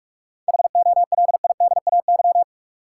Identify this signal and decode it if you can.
Morse code, heard as a keyed tone, about 43 wpm, 700 Hz. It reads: HOLIDAY
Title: Seven Letter Words - 45wpm